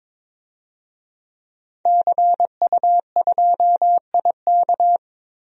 Morse code, heard as a keyed tone, about 22 wpm, 700 Hz. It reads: CU2IK
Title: DXCC-100 Call Signs - 5 Characters - Repeated 2x - 22wpm